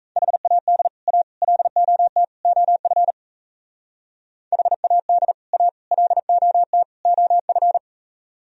Morse code, heard as a keyed tone, about 33 wpm, 700 Hz. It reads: HAD A LOT OF HAD A LOT OF
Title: Speed-Racing - Sets of 4 Words 28wpm